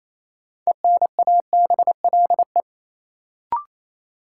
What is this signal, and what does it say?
Morse code, heard as a keyed tone, about 27 wpm, 700 Hz. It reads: ENABLE
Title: Sentences from Top 1500 Words - New Word Drill Alt - 28wpm